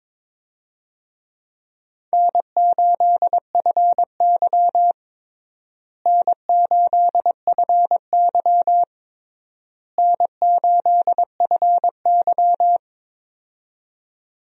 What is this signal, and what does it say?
Morse code, heard as a keyed tone, about 21 wpm, 700 Hz. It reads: N8FY N8FY N8FY
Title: USA Call Signs - 4 Characters - Repeated 3x - 22wpm